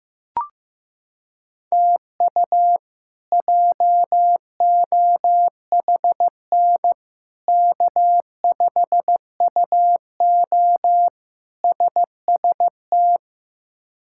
Morse code, heard as a keyed tone, about 16 wpm, 700 Hz. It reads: TU JOHN K5UO SST
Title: K1USN SST QSO Practice - 15wpm